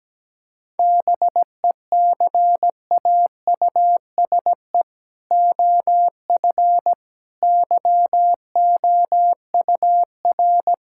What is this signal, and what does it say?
Morse code, heard as a keyed tone, about 17 wpm, 700 Hz. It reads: BECAUSE OF YOUR
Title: Sentences from Top 1000 Words - Review Part 5 - 17wpm